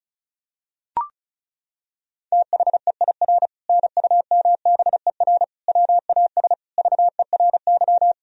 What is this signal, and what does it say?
Morse code, heard as a keyed tone, about 36 wpm, 700 Hz. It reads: THEIR NUMBER WAS VERY
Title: Sentences from Top 1000 Words - Review Part 1 - 35wpm